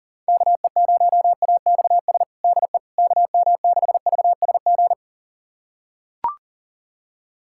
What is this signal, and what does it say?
Morse code, heard as a keyed tone, about 37 wpm, 700 Hz. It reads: KE0AXS DE KM6VSG
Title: USA Callsign DE Callsign 40wpm